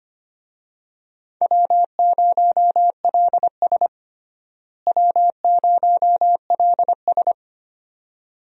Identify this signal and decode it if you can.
Morse code, heard as a keyed tone, about 25 wpm, 700 Hz. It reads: W0LS W0LS
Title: USA Call Signs - 4 Characters - Repeated 2x - 25wpm